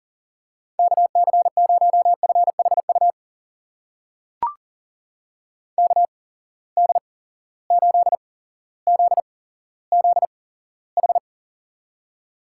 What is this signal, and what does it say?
Morse code, heard as a keyed tone, about 38 wpm, spaced as for 17 wpm, 700 Hz. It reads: KC0FHU KD8ZZH
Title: USA Call Signs - 6 Characters - 8x Character Spacing - 40wpm